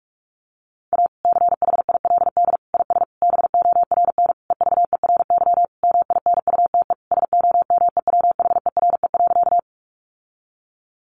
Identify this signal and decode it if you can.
Morse code, heard as a keyed tone, about 44 wpm, 700 Hz. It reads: A CHILD IS BORN EVERY MINUTE SOMEWHERE.
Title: Sentences from Top 1400 Words 45wpm